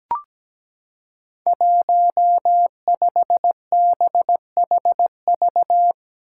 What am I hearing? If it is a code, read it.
Morse code, heard as a keyed tone, about 17 wpm, 700 Hz. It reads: 15BHV
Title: Five Challenging Letter-Number Combinations 17wpm